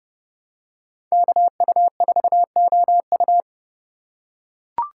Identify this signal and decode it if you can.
Morse code, heard as a keyed tone, about 30 wpm, 700 Hz. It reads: KU4OU